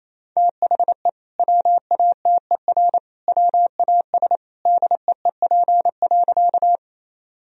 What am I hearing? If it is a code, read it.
Morse code, heard as a keyed tone, about 28 wpm, 700 Hz. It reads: THE WATER WAS DEEP.
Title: Sentences from Top 1000 Words - Review Part 3 - 28wpm